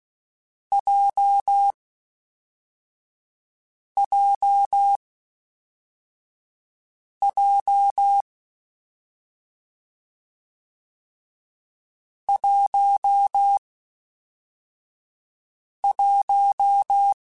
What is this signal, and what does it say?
Morse code, heard as a keyed tone, about 16 wpm, 800 Hz. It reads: J J J 1 1
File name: Difficult Combos / Difficult Combos-mp3s / WPJ WPJ1 @5